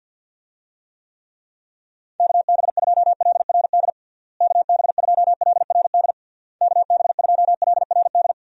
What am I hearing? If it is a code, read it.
Morse code, heard as a keyed tone, about 50 wpm, 700 Hz. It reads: KB2LRD KB2LRD KB2LRD
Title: USA Call Signs - 6 Characters - Repeated 3x - 50wpm